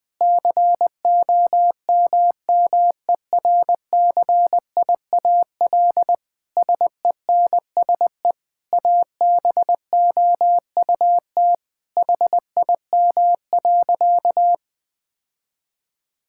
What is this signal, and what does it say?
Morse code, heard as a keyed tone, about 20 wpm, 700 Hz. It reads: COMMERCIAL SENSE ABOUT HIM.
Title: Sentences from Top 1000 Words - Review Part 5 - 20wpm